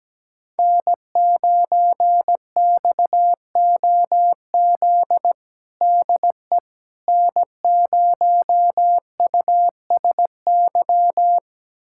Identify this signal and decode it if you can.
Morse code, heard as a keyed tone, about 17 wpm, 700 Hz. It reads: N9XOZ DE N0USY
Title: USA Callsign DE Callsign - 5 Characters - 17wpm